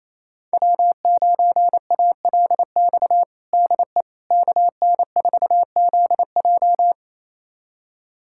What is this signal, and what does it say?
Morse code, heard as a keyed tone, about 28 wpm, 700 Hz. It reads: W9ALX DE KN4ZJ
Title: USA Callsign DE Callsign - 5 Characters - 28wpm